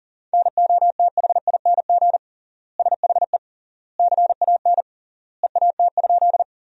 Morse code, heard as a keyed tone, about 40 wpm, 700 Hz. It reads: NOTHING SHE CAN EAT?
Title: 3x Word-Spacing - Sentences from Top 500 Words - 40wpm